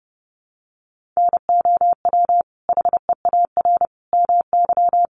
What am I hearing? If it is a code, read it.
Morse code, heard as a keyed tone, about 30 wpm, 700 Hz. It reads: NOW HEAR MY
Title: Sentences from Top 1100 Words 30wpm